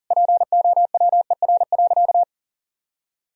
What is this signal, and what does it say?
Morse code, heard as a keyed tone, about 40 wpm, 700 Hz. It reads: POWER.
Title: Sentences from Top 1000 Words - Review Part 4 - 40wpm